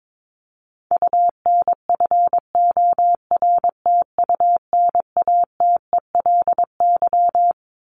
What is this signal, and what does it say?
Morse code, heard as a keyed tone, about 22 wpm, 700 Hz. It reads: UNFORTUNATELY
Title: LY Suffix - Top 100 Words with Base Words 22wpm